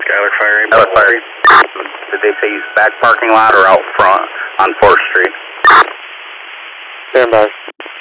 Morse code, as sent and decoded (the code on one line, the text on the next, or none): none